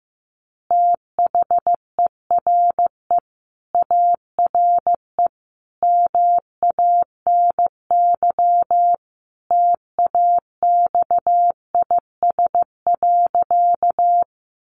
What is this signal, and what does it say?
Morse code, heard as a keyed tone, about 15 wpm, 700 Hz. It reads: THERE ARE MANY TAXIS.
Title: Easy Phrases 15wpm